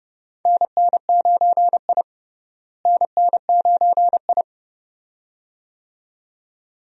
Morse code, heard as a keyed tone, about 31 wpm, 700 Hz. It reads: NN9I NN9I
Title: USA Call Signs - 4 Characters - Repeated 2x - 30wpm